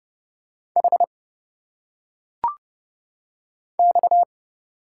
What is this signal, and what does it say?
Morse code, heard as a keyed tone, about 30 wpm, 700 Hz. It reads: H X
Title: Single Letters - Phonetic Pronunciation 30wpm